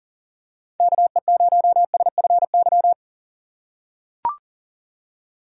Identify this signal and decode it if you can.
Morse code, heard as a keyed tone, about 40 wpm, 700 Hz. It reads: KE0SFY
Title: USA Call Signs - 6 Characters - 3x Character Spacing - 40wpm